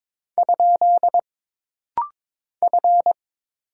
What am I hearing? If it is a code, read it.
Morse code, heard as a keyed tone, about 22 wpm, 700 Hz. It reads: ? F
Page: This rapid-fire timing is geared to push your ICR (Instant Character Recognition) skills.